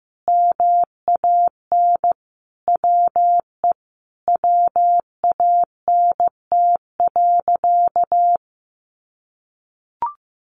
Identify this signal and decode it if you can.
Morse code, heard as a keyed tone, about 15 wpm, 700 Hz. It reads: MAN WE WANT.
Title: Sentences from Top 200 Words 15wpm